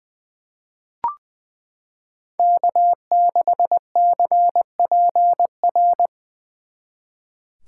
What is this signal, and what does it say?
Morse code, heard as a keyed tone, about 20 wpm, 700 Hz. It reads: K6CPR
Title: 2023 SOTA Chaser Call Signs - 20wpm